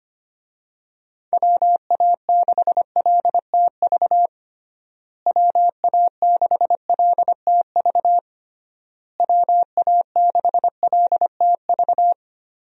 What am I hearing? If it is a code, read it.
Morse code, heard as a keyed tone, about 25 wpm, 700 Hz. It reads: WA6LTV WA6LTV WA6LTV
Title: USA Call Signs - 6 Characters - Repeated 3x - 25wpm